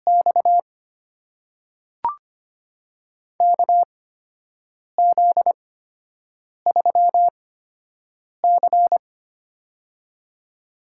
Morse code, heard as a keyed tone, about 25 wpm, spaced as for 7 wpm, 700 Hz. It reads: X KZ3C
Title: USA Call Signs - 4 Characters - 8x Character Spacing - 25wpm